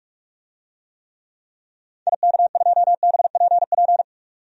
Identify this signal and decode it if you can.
Morse code, heard as a keyed tone, about 44 wpm, 700 Hz. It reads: IK2BPP
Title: DXCC-100 Call Signs - 6 Characters - 45wpm